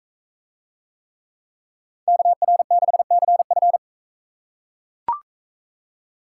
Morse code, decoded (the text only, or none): KR6CF